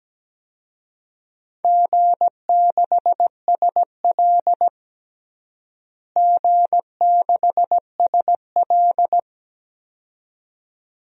Morse code, heard as a keyed tone, about 17 wpm, 700 Hz. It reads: G6SL G6SL
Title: DXCC-100 Call Signs - 4 Characters - Repeated 2x - 17wpm